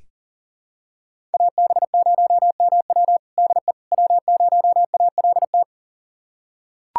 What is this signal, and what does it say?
Morse code, heard as a keyed tone, about 40 wpm, 700 Hz. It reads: AB0MW DE W0ALT
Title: USA Callsign DE Callsign - 5 Characters - 40wpm